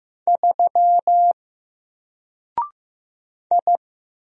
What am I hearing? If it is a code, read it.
Morse code, heard as a keyed tone, about 15 wpm, 700 Hz. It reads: 3 I
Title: Single Letter-Number 15wpm